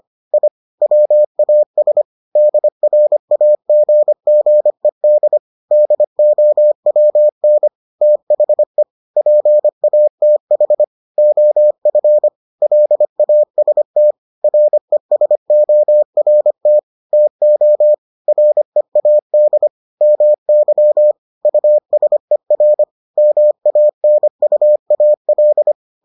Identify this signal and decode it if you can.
Morse code, heard as a keyed tone, about 25 wpm, 600 Hz. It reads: I WAS DRAGGED DOWN THE PATH OF LAST RESORT TO READ MY USER MANUAL